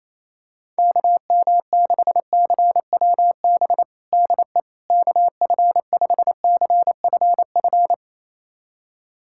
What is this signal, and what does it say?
Morse code, heard as a keyed tone, about 28 wpm, 700 Hz. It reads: KM6CWB DE KF5CFF
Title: USA Callsign DE Callsign 28wpm